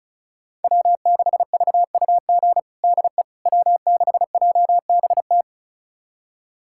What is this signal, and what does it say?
Morse code, heard as a keyed tone, about 34 wpm, 700 Hz. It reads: W6VUG DE W6JBT
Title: USA Callsign DE Callsign - 5 Characters - 35wpm